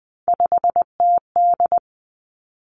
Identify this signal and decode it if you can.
Morse code, heard as a keyed tone, about 20 wpm, 700 Hz. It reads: 5TD
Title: Three Challenging Letter-Number Combinations 20wpm